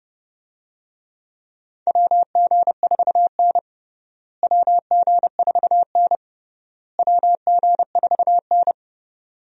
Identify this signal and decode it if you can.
Morse code, heard as a keyed tone, about 30 wpm, 700 Hz. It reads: WG4N WG4N WG4N
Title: USA Call Signs - 4 Characters - Repeated 3x - 30wpm